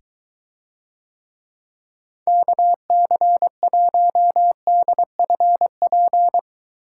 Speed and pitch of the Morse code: 23 wpm, 700 Hz

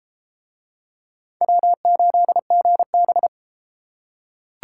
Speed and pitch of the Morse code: 30 wpm, 700 Hz

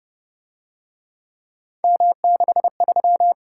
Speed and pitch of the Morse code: 29 wpm, 700 Hz

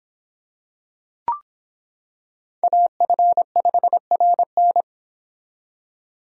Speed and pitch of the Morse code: 25 wpm, 700 Hz

